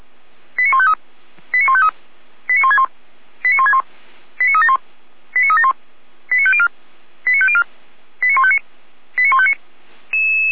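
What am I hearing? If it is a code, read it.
Five-tone selective call (ZVEI), heard as quick runs of five tones, tones of about 70 ms. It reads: 89153 89153 89161 89161 89371 89371 89504 89504 89159 89159
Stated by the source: Five-tone ZVEI selective call received by SDR, from the feuerwehr-orlamuende.org fire brigade server.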